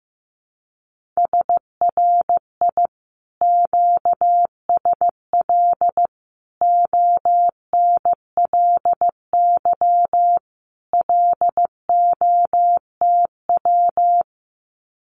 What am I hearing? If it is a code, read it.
Morse code, heard as a keyed tone, about 15 wpm, 700 Hz. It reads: SRI QSL ONLY LOTW
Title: Common QSO Phrases Encore 15wpm